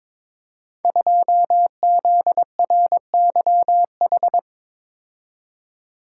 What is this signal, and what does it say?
Morse code, heard as a keyed tone, about 22 wpm, 700 Hz. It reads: 2ZRYH
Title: Code Groups - 5 Letter:Numbers - 22wpm